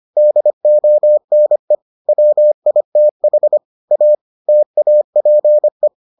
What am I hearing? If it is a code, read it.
Morse code, heard as a keyed tone, about 25 wpm, 600 Hz. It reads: DONE WITH A TAPE